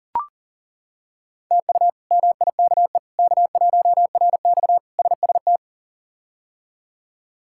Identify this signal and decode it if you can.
Morse code, heard as a keyed tone, about 40 wpm, 700 Hz. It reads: TU MIKE K1RX SST
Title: K1USN SST QSO Practice - 40wpm